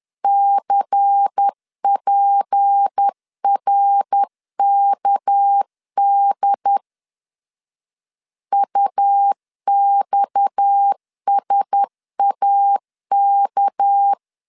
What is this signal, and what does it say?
Morse code, heard as a keyed tone, about 10 wpm, 800 Hz. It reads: CPRKD UXSAK
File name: LEZIONI / 6 PKX0 / 6 PKX 10wpm 2426